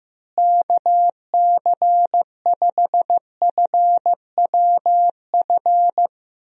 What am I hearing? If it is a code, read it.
Morse code, heard as a keyed tone, about 15 wpm, 700 Hz. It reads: KC5FWF
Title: USA Call Signs - 6 Characters - Repeated 2x - 15wpm